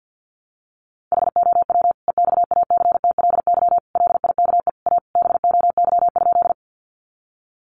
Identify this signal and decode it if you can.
Morse code, heard as a keyed tone, about 50 wpm, 700 Hz. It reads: HOW EXACTLY LIKE A BOY?
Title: Sentences from Top 900 Words 50wpm